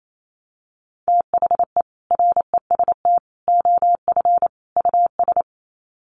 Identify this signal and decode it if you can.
Morse code, heard as a keyed tone, about 28 wpm, 700 Hz. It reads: THE REST OF US
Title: Sets of 4 Words 28wpm